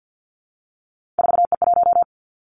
Morse code, heard as a keyed tone, about 50 wpm, 700 Hz. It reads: 4E1
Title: Three Challenging Letter-Number Combinations 50wpm